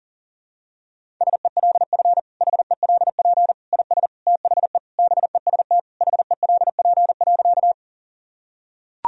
Morse code, heard as a keyed tone, about 40 wpm, 700 Hz. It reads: SELF HELP IS THE BEST HELP.
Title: Sentences from Top 1000 Words - Review Part 1 - 40wpm